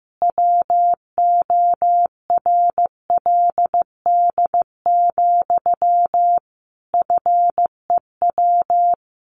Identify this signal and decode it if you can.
Morse code, heard as a keyed tone, about 15 wpm, 700 Hz. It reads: WORLD, FEW
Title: Sentences from Top 1400 Words 15wpm